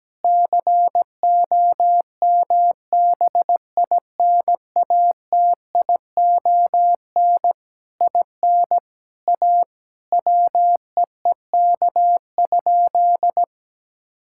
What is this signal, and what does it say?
Morse code, heard as a keyed tone, about 17 wpm, 700 Hz. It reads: COMBINATION IN A WEEK?
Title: Sentences from Top 1000 Words - Review Part 6 - 17wpm